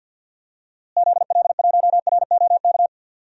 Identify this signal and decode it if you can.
Morse code, heard as a keyed tone, about 51 wpm, 700 Hz. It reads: ZL1ROK
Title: DXCC-100 Call Signs - 6 Characters - Repeated 3x - 50wpm